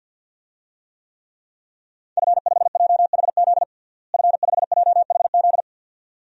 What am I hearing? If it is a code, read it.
Morse code, heard as a keyed tone, about 50 wpm, 700 Hz. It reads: F5JHZ F5JHZ
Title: DXCC-100 Call Signs - 5 Characters - Repeated 2x - 50wpm